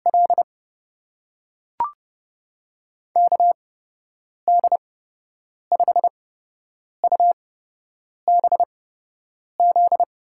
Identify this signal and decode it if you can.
Morse code, heard as a keyed tone, about 30 wpm, spaced as for 8 wpm, 700 Hz. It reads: L KD5UBZ
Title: USA Call Signs - 8x Character Spacing - 30wpm